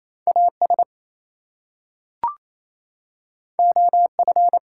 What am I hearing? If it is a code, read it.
Morse code, heard as a keyed tone, about 28 wpm, 700 Hz. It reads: AS OF
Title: Two Letter Words - 28wpm